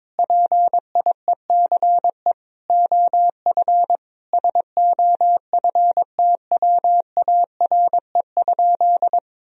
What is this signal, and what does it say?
Morse code, heard as a keyed tone, about 22 wpm, 700 Hz. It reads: PIECE OF SOFTWARE?
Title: Sentences from Top 1000 Words - Review Part 3 - 22wpm